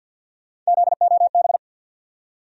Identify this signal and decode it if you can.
Morse code, heard as a keyed tone, about 53 wpm, 700 Hz. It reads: ZOB
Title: USA Call Sign Postfixes - 3 Letters 50wpm